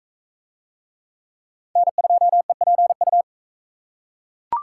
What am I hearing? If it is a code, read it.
Morse code, heard as a keyed tone, about 40 wpm, 700 Hz. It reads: N2EPU